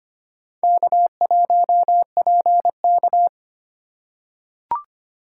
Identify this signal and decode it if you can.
Morse code, heard as a keyed tone, about 25 wpm, 690 Hz. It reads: K1PK
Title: USA Call Signs - 4 Characters - 4x Character Spacing - 25wpm